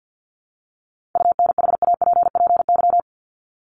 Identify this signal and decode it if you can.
Morse code, heard as a keyed tone, about 50 wpm, 700 Hz. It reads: UNHAPPY
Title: Top 1501-1750 Words 50wpm